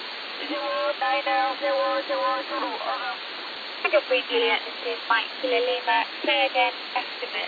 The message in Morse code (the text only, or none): none